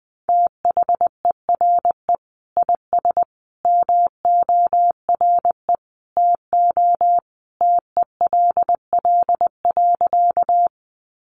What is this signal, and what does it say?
Morse code, heard as a keyed tone, about 20 wpm, 700 Hz. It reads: THERE IS MORE TO TELL.
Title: Sentences from Top 200 Words 20wpm